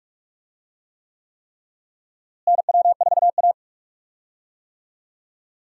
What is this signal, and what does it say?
Morse code, heard as a keyed tone, about 42 wpm, 700 Hz. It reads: NW4A
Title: USA Call Signs - 4 Characters - Repeated 3x - 45wpm